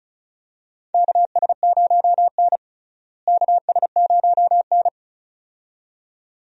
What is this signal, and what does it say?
Morse code, heard as a keyed tone, about 34 wpm, 700 Hz. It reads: KS0N KS0N
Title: USA Call Signs - 4 Characters - Repeated 2x - 35wpm